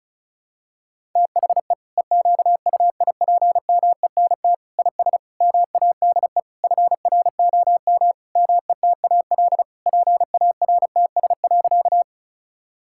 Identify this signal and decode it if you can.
Morse code, heard as a keyed tone, about 34 wpm, 700 Hz. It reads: THE EQUIPMENT IS MADE FROM METAL PARTS.
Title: Sentences from Top 1000 Words - Review Part 6 - 35wpm